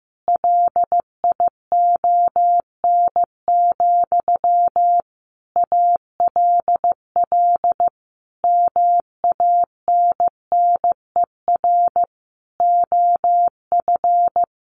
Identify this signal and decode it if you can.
Morse code, heard as a keyed tone, about 14 wpm, 700 Hz. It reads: LION, ALL MANNER OF
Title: A Sound Of Thunder 15wpm